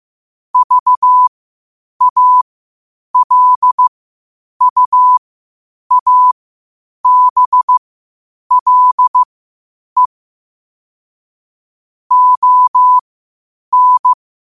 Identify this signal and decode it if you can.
Morse code, heard as a keyed tone, about 15 wpm, spaced as for 8 wpm, 1000 Hz. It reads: VALUABLE ON
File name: qotdfiles / 8wpm